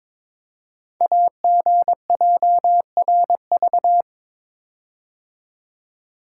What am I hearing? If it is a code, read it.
Morse code, heard as a keyed tone, about 22 wpm, 700 Hz. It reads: AGJRV
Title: Code Groups - 5 Letter:Numbers - 22wpm